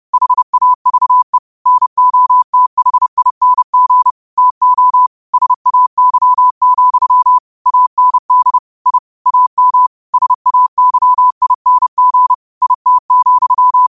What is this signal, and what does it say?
Morse code, heard as a keyed tone, about 30 wpm, 1000 Hz. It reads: HAVE NOTHING TO SAY, AND I AM SAYING IT,
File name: qotdfiles / 30wpm